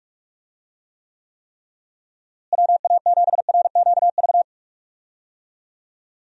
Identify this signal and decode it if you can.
Morse code, heard as a keyed tone, about 41 wpm, 700 Hz. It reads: WA7RQV
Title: USA Call Signs - 6 Characters - Repeated 3x - 45wpm